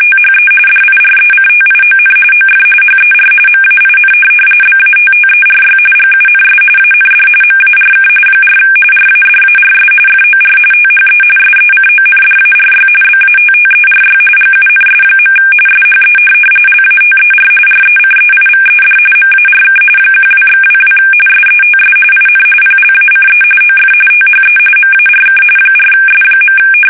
RTTY, about 45.45 baud. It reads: THE DIRECTION OF HER MAIN BATTERY GUNS AND MARK 4 RADARS FOR THE SECONDARY BATTERY GUNS. SHE LATER RECEIVED MARK 12/22 SETS IN PLACE OF THE MARK 4S.
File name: fsk850 / USS-Massachusetts